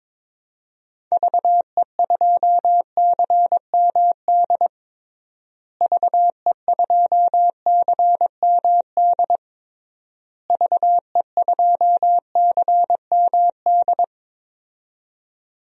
Morse code, heard as a keyed tone, about 22 wpm, 700 Hz. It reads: VE2CMD VE2CMD VE2CMD
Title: DXCC-100 Call Signs - 6 Characters - Repeated 3x - 22wpm